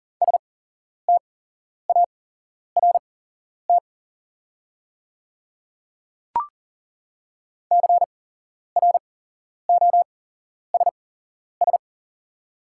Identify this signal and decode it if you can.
Morse code, heard as a keyed tone, about 40 wpm, spaced as for 11 wpm, 700 Hz. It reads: START CROSS
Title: 8x Character Spacing - Top 500 Words - 40wpm